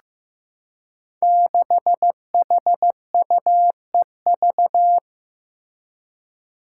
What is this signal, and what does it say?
Morse code, heard as a keyed tone, about 15 wpm, 700 Hz. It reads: 6HUEV
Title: Five Challenging Letter-Number Combinations 15wpm